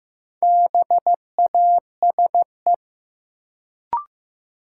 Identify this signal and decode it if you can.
Morse code, heard as a keyed tone, about 15 wpm, 700 Hz. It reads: BASE
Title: Sentences from Top 1000 Words - Review Part 1 - New Word Drill - 15wpm